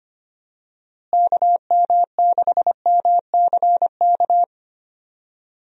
Morse code, heard as a keyed tone, about 25 wpm, 700 Hz. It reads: KM6MCK
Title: USA Call Signs - 6 Characters - Repeated 2x - 25wpm